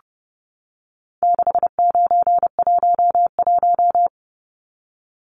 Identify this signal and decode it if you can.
Morse code, heard as a keyed tone, about 30 wpm, 700 Hz. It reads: 6911
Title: Numbers - 4-Digit 30wpm